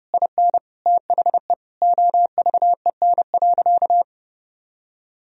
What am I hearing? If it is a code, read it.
Morse code, heard as a keyed tone, about 30 wpm, 700 Hz. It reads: IN THE OVEN.
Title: Sentences from Top 1000 Words - Review Part 4 - 30wpm